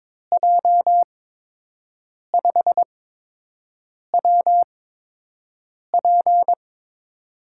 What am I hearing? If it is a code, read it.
Morse code, heard as a keyed tone, about 22 wpm, spaced as for 6 wpm, 700 Hz. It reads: J5WP
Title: USA Call Signs - 5 Characters - 8x Character Spacing - 22wpm